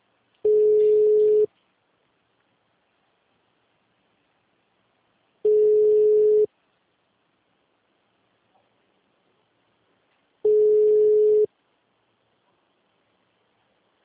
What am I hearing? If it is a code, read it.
Ringback tone (Europe, 425 Hz).
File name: out- -2038- - - .86